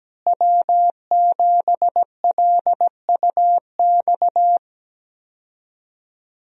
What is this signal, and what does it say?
Morse code, heard as a keyed tone, about 17 wpm, 700 Hz. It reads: W7LUX
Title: USA Call Signs - 5 Characters - Repeated 3x - 17wpm